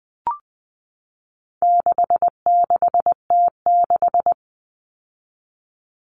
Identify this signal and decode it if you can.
Morse code, heard as a keyed tone, about 20 wpm, 700 Hz. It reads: 66T6
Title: Common Cut Numbers - 4-Digit 20wpm